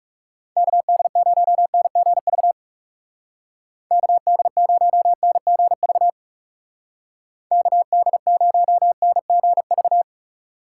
Morse code, heard as a keyed tone, about 36 wpm, 700 Hz. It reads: KD0NGV KD0NGV KD0NGV
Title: Speed Racing - USA 2x3 Call Signs 30wpm